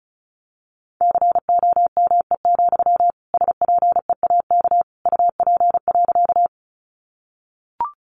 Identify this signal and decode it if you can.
Morse code, heard as a keyed tone, about 34 wpm, 700 Hz. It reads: COME, SPEAK UP.
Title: Sentences from Top 1100 Words 35wpm